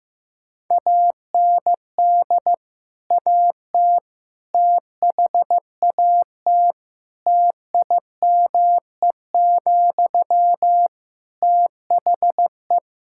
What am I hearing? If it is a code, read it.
Morse code, heard as a keyed tone, about 15 wpm, 700 Hz. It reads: AND AT THAT TIME, THE
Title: Sentences from Top 1000 Words - Review Part 5 - 15wpm